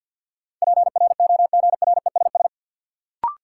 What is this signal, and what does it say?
Morse code, heard as a keyed tone, about 52 wpm, 700 Hz. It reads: PROGRESS
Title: Sentences from Top 1400 Words - New Word Drill - 50wpm